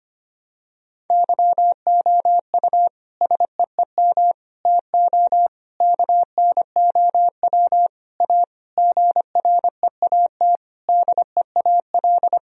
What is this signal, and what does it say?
Morse code, heard as a keyed tone, about 25 wpm, 700 Hz. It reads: YOU SEEM TO KNOW A GREAT DEAL
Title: Sentences from Top 1000 Words - Review Part 1 - 25wpm